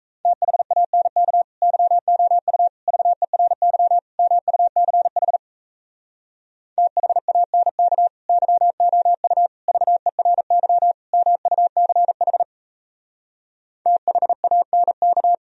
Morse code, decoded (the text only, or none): THANK YOU VERY MUCH THANK YOU VERY MUCH THANK